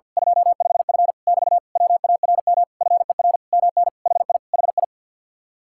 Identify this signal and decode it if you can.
Morse code, heard as a keyed tone, about 50 wpm, 700 Hz. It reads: 25F <BT> WARM FER MN HI HI
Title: Speed-Racing Common QSO Phrases - 50wpm